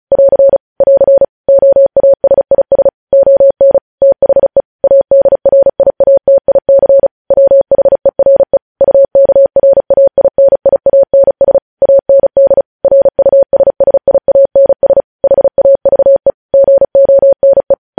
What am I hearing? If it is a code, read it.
Morse code, heard as a keyed tone, about 36 wpm, 550 Hz. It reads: <AR> <AR> OASIS ON THE ADRIATIC WHERE UKRAINIANS AND RUSSIANS HAVE GONE